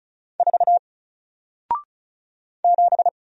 Morse code, heard as a keyed tone, about 36 wpm, 700 Hz. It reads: 4 7
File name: Single-Letter-Number-35wpm